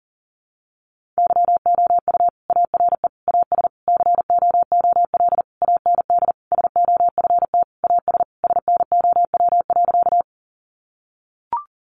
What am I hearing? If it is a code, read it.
Morse code, heard as a keyed tone, about 40 wpm, 700 Hz. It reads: YOU ARE AS COOL AND SOFT AS SNOW.
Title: Sentences from Top 1500 Words 40wpm